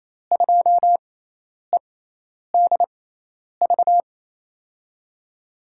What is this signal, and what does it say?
Morse code, heard as a keyed tone, about 28 wpm, spaced as for 10 wpm, 700 Hz. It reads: 2EDV
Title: USA Call Signs - 6 Characters - 6x Character Spacing - 28wpm